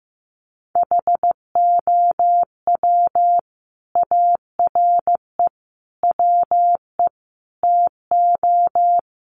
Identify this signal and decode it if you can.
Morse code, heard as a keyed tone, about 15 wpm, 700 Hz. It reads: HOW ARE WE TO
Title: Sentences from Top 1300 Words 15wpm